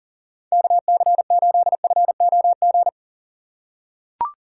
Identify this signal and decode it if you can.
Morse code, heard as a keyed tone, about 40 wpm, 700 Hz. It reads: KC8FOG
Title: USA Call Signs - 2x Character Spacing - 40wpm